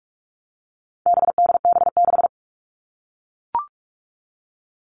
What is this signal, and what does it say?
Morse code, heard as a keyed tone, about 44 wpm, 700 Hz. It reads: BDB6
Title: All Cut Numbers - 4-Digit 45wpm